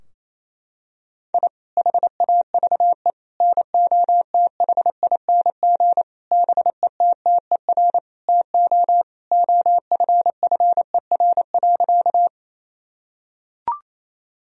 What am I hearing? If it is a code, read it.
Morse code, heard as a keyed tone, about 28 wpm, 700 Hz. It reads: I HAVE NOTHING BETTER TO OFFER.
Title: Sentences from Top 1000 Words - Review Part 4 - 28wpm